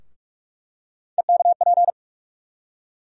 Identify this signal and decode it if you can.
Morse code, heard as a keyed tone, about 42 wpm, 700 Hz. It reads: EKP
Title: USA Call Sign Postfixes - 3 Letters 45wpm